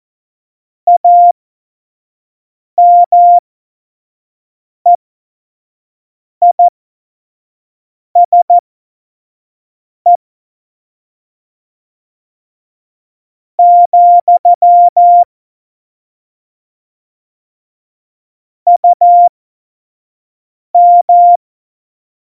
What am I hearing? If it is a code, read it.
Morse code, heard as a keyed tone, about 14 wpm, spaced as for 4 wpm, 700 Hz. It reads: AMEISE, UM
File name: Morsekurs / Witze / Witz 15. Mai 2018, 25 F70